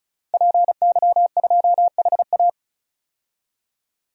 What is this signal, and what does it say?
Morse code, heard as a keyed tone, about 36 wpm, 700 Hz. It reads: PY2HA
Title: DXCC-100 Call Signs - 5 Characters - 35wpm